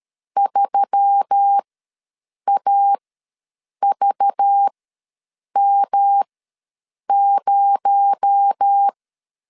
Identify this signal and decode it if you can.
Morse code, heard as a keyed tone, about 13 wpm, spaced as for 7 wpm, 800 Hz. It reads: 3AVM0
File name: LEZIONI / 7 GF9Z / 7 GF ws 12wpm 3140